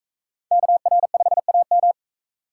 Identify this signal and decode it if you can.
Morse code, heard as a keyed tone, about 43 wpm, 700 Hz. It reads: KR5AM